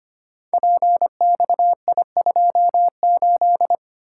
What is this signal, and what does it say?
Morse code, heard as a keyed tone, about 25 wpm, 700 Hz. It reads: PXI28
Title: Code Groups - 5 Letter:Numbers - 25wpm